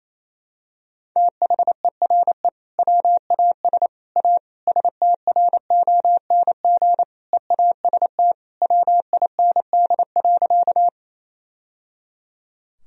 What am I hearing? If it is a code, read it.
Morse code, heard as a keyed tone, about 28 wpm, 700 Hz. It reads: THERE WAS A STRONG EAST WIND.
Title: Sentences from Top 1000 Words - Review Part 3 - 28wpm